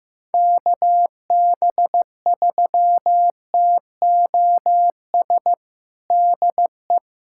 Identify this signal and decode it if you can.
Morse code, heard as a keyed tone, about 16 wpm, 700 Hz. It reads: KB3TOS DE
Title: USA Callsign DE Callsign - 6 Characters - 15wpm